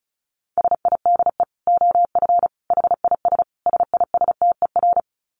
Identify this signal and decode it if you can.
Morse code, heard as a keyed tone, about 34 wpm, 700 Hz. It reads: SIDE OF HIS SISTER
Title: Dit Challenge 35wpm